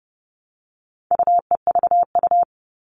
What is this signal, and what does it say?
Morse code, heard as a keyed tone, about 30 wpm, 700 Hz. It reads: UEVU
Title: All Cut Numbers - 4-Digit 30wpm